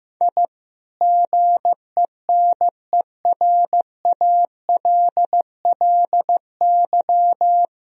Morse code, heard as a keyed tone, about 15 wpm, 700 Hz. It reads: I GENERALLY
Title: Sentences from Top 1000 Words - Review Part 4 - 15wpm